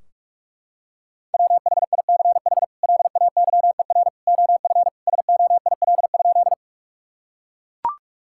Morse code, heard as a keyed tone, about 41 wpm, 700 Hz. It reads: WHICH LAYER OF SOIL?
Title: Sentences from Top 1600 Words 45wpm